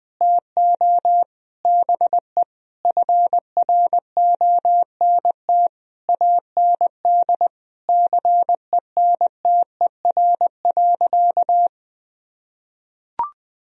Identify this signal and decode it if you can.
Morse code, heard as a keyed tone, about 21 wpm, 700 Hz. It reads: TO BE FRONT AND CENTER.
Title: Sentences from Top 1000 Words - Review Part 2 - 20wpm